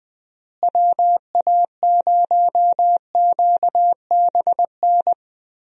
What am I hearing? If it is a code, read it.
Morse code, heard as a keyed tone, about 21 wpm, 700 Hz. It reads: WA0QBN
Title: USA Call Signs - 6 Characters - Repeated 3x - 20wpm